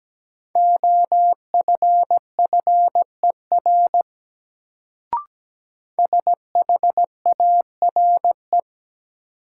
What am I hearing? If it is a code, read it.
Morse code, heard as a keyed tone, about 17 wpm, 700 Hz. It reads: OFFER SHARE
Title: Sentences from Top 900 Words - New Word Drill - 17wpm